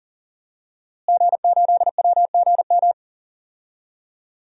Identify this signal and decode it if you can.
Morse code, heard as a keyed tone, about 40 wpm, 700 Hz. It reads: G8WGM
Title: DXCC-100 Call Signs - 5 Characters - Repeated 3x - 40wpm